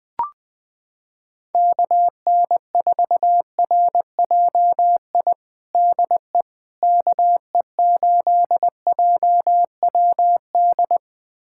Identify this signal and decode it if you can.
Morse code, heard as a keyed tone, about 20 wpm, 700 Hz. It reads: KN4RJI DE KE8JWD
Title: USA Callsign DE Callsign - 6 Characters - 20wpm